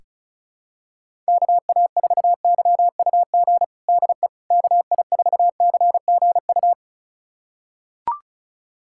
Morse code, heard as a keyed tone, about 34 wpm, 700 Hz. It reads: KA4YUG DE KI4CGU
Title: USA Callsign DE Callsign - 6 Characters - 35wpm